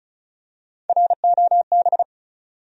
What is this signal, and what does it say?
Morse code, heard as a keyed tone, about 36 wpm, 700 Hz. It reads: ROB
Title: USA Common Names - Max 4 Letters - 35wpm